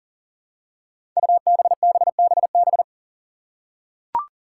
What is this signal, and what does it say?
Morse code, heard as a keyed tone, about 40 wpm, 700 Hz. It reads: UBBBB
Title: Five Challenging Letter-Number Combinations 40wpm